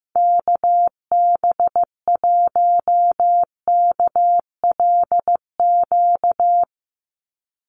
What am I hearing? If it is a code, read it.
Morse code, heard as a keyed tone, about 15 wpm, 700 Hz. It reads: KB1KLQ
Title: USA Call Signs 15wpm